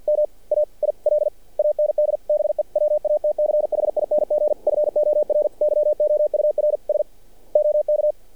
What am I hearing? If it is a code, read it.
Morse code, heard as a keyed tone, about 51 wpm, 600 Hz. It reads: MAIL AND BE WATCHING FOR YOUR S OK
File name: archived / CRSnet